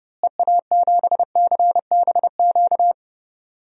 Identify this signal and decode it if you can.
Morse code, heard as a keyed tone, about 30 wpm, 700 Hz. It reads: EA7CBQ
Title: DXCC-100 Call Signs - 6 Characters - Repeated 3x - 30wpm